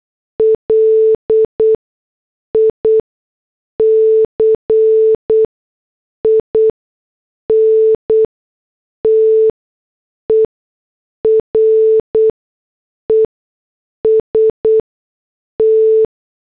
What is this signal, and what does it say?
Morse code, heard as a keyed tone, about 8 wpm, 440 Hz. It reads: L I C I N T E R E S T
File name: morse / headlines